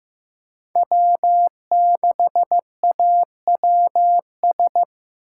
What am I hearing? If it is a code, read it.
Morse code, heard as a keyed tone, about 15 wpm, 700 Hz. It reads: W6AWS
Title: K1USN SST QSO Practice - 15wpm